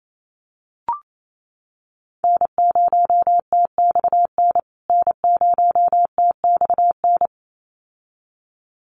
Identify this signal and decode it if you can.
Morse code, heard as a keyed tone, about 28 wpm, 700 Hz. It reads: N0TXN N0TXN
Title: USA Call Signs Repeated 2x 28wpm